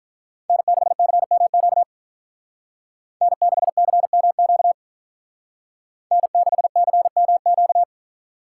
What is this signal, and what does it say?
Morse code, heard as a keyed tone, about 42 wpm, 700 Hz. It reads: N6CMQ N6CMQ N6CMQ
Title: Speed Racing - USA 1x3 Call Signs 35wpm